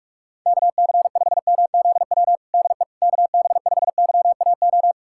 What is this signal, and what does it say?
Morse code, heard as a keyed tone, about 46 wpm, 700 Hz. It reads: KC5MZW DE KB5YAO
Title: USA Callsign DE Callsign - 6 Characters - 45wpm